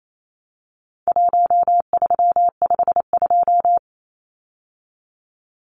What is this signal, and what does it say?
Morse code, heard as a keyed tone, about 28 wpm, 700 Hz. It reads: 1352
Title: Numbers - Contest Style - 4-Digit 28wpm